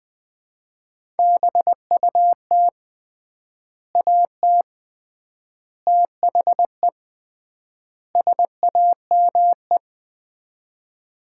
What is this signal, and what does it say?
Morse code, heard as a keyed tone, about 21 wpm, 700 Hz. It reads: BUT AT THE SAME
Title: 3x Word-Spacing - Sets of 4 Words - 20wpm